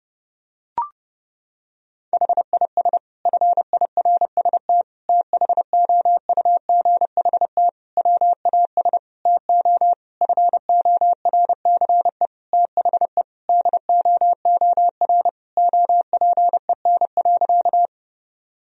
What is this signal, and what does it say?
Morse code, heard as a keyed tone, about 30 wpm, 700 Hz. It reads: HIS FIRST THOUGHT WAS TO FORCE THE DOOR OPEN.
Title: Sentences from Top 1000 Words - Review Part 3 - 30wpm